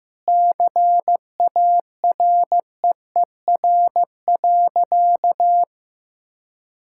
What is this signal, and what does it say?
Morse code, heard as a keyed tone, about 16 wpm, 700 Hz. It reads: CAREER.
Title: Sentences from Top 1000 Words - Review Part 6 - 15wpm